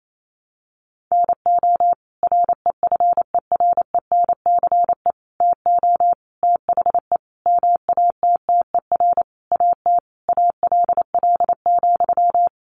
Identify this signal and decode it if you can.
Morse code, heard as a keyed tone, about 28 wpm, 700 Hz. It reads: NO REFERENCE TO THE MATTER AT ALL,
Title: Sentences from Top 1200 Words 28wpm